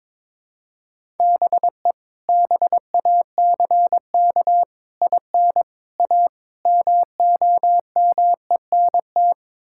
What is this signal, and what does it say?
Morse code, heard as a keyed tone, about 22 wpm, 700 Hz. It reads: BE BACK IN A MOMENT
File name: practice / 22wpm / Build A Phrase - 5-Word Phrases - 22wpm